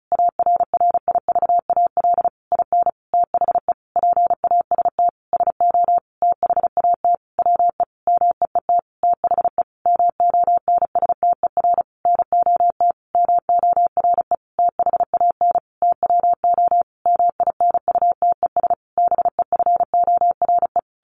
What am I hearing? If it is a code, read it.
Morse code, heard as a keyed tone, about 34 wpm, 700 Hz. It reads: ARRIVAL IN THE PAST SO THAT WE MEET THE MONSTER NOT MORE THAN TWO MINUTES BEFORE
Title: A Sound Of Thunder 35wpm